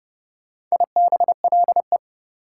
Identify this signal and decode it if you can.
Morse code, heard as a keyed tone, about 39 wpm, 700 Hz. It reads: IBLE